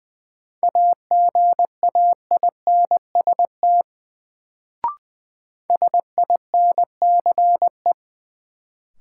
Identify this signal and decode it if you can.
Morse code, heard as a keyed tone, about 20 wpm, 700 Hz. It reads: AGAINST SINCE
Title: Sets of 2 Words - New Word Drill - 20wpm